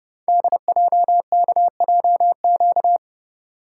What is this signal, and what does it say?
Morse code, heard as a keyed tone, about 30 wpm, 700 Hz. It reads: DJKJQ
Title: Code Groups - 5 Letters - 30wpm